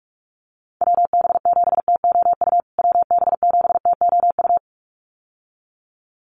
Morse code, heard as a keyed tone, about 44 wpm, 700 Hz. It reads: WB7TOU WB7TOU
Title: USA Call Signs Repeated 2x 45wpm